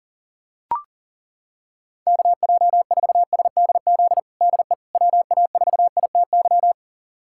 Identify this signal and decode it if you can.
Morse code, heard as a keyed tone, about 40 wpm, 700 Hz. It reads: KJ4SDZ DE WA4ITY
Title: USA Callsign DE Callsign - 6 Characters - 40wpm